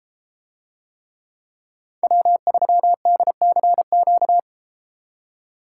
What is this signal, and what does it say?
Morse code, heard as a keyed tone, about 33 wpm, 700 Hz. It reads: W3DCQ